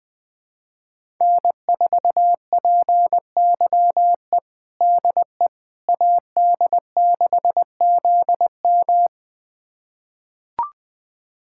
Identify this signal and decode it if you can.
Morse code, heard as a keyed tone, about 20 wpm, 700 Hz. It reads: N4PYE DE AD6ZM
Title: USA Callsign DE Callsign - 5 Characters - 20wpm